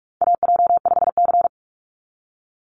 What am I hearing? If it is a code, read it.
Morse code, heard as a keyed tone, about 44 wpm, 700 Hz. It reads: AJ5C
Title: NA SOTA Chaser Callsigns 45wpm